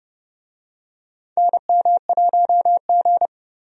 Morse code, heard as a keyed tone, about 30 wpm, 700 Hz. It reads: NM1G
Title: USA Call Signs - 4 Characters - 30wpm